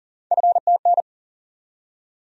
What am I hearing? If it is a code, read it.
Morse code, heard as a keyed tone, about 40 wpm, 700 Hz. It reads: FTN
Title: USA Call Sign Postfixes - 3 Letters 40wpm